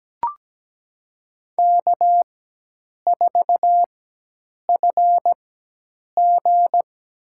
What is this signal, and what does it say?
Morse code, heard as a keyed tone, about 17 wpm, spaced as for 8 wpm, 700 Hz. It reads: K4FG
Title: USA Call Signs - 4x Character Spacing - 17wpm